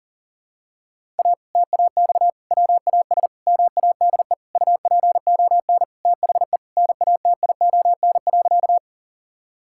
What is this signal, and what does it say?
Morse code, heard as a keyed tone, about 37 wpm, 700 Hz. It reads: A TAX WAS MADE UPON THE NATION.
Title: Sentences from Top 1000 Words - Review Part 6 - 40wpm